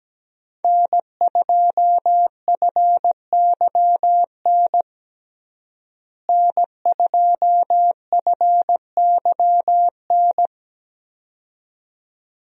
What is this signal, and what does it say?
Morse code, heard as a keyed tone, about 17 wpm, 700 Hz. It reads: N2FYN N2FYN
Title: USA Call Signs - 5 Characters - Repeated 2x - 17wpm